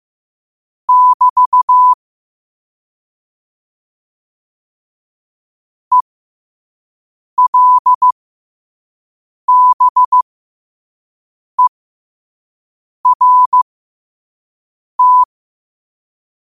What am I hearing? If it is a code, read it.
Morse code, heard as a keyed tone, about 15 wpm, spaced as for 5 wpm, 1000 Hz. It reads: <BT> ELBERT
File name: 5wpm